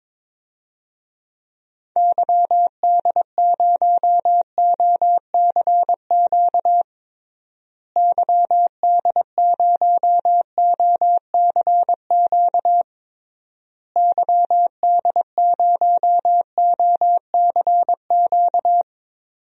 Morse code, heard as a keyed tone, about 22 wpm, 700 Hz. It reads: YD0OCQ YD0OCQ YD0OCQ
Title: DXCC-100 Call Signs - 6 Characters - Repeated 3x - 22wpm